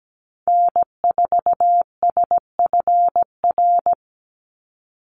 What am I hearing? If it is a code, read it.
Morse code, heard as a keyed tone, about 17 wpm, 700 Hz. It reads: N4SFR
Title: NA SOTA Activator Callsigns 17wpm